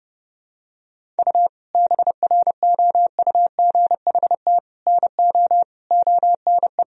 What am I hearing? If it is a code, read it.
Morse code, heard as a keyed tone, about 32 wpm, 700 Hz. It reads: U BROUGHT NO ONE
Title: Sentences from Top 500 Words and Abbreviations 30wpm